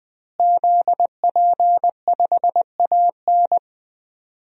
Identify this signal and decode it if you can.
Morse code, heard as a keyed tone, about 20 wpm, 700 Hz. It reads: ZP5AN
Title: DXCC-100 Call Signs - 5 Characters - Repeated 3x - 20wpm